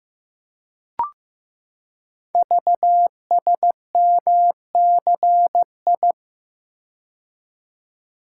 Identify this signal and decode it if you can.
Morse code, heard as a keyed tone, about 15 wpm, 700 Hz. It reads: VSMCI
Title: Code Groups - 5 Letter:Numbers - 15wpm